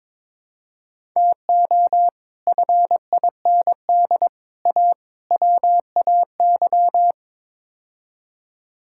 Morse code, heard as a keyed tone, about 22 wpm, 700 Hz. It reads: TO FIND A WAY
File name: practice / 22wpm / Build A Phrase - 5-Word Phrases - 22wpm